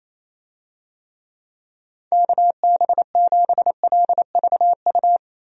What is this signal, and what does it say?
Morse code, heard as a keyed tone, about 27 wpm, 700 Hz. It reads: KB7LVU
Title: USA Call Signs - 6 Characters - Repeated 3x - 28wpm